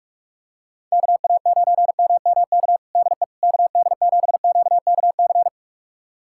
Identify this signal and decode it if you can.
Morse code, heard as a keyed tone, about 43 wpm, 700 Hz. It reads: KA9MMK DE KD7QKC
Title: USA Callsign DE Callsign 45wpm